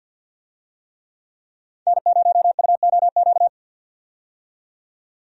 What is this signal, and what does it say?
Morse code, heard as a keyed tone, about 51 wpm, 700 Hz. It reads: N0UOQ
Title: USA Call Signs - 5 Characters - 50wpm